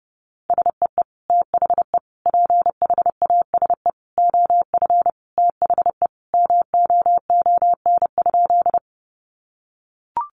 Morse code, heard as a keyed tone, about 30 wpm, 700 Hz. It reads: SEE THE PHASE OF THE MOON?
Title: Sentences from Top 1200 Words 30wpm